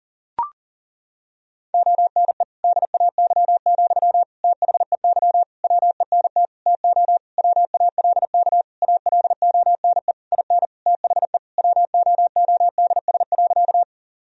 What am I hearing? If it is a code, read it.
Morse code, heard as a keyed tone, about 40 wpm, 700 Hz. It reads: ONE DAY, THEY WENT TO WALK ALONE IN THE WOODS.
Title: Sentences from Top 1000 Words - Review Part 2 - 40wpm